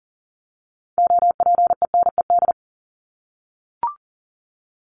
Morse code, heard as a keyed tone, about 40 wpm, 700 Hz. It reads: OPENED
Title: ED Suffix - Top 100 Words with Base Words 40wpm